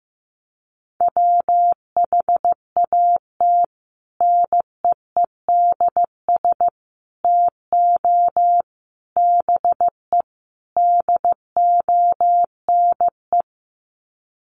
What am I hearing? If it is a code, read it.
Morse code, heard as a keyed tone, about 15 wpm, 700 Hz. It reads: WHAT NEEDS TO BE DONE
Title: Sets of 5 Words Encore 15wpm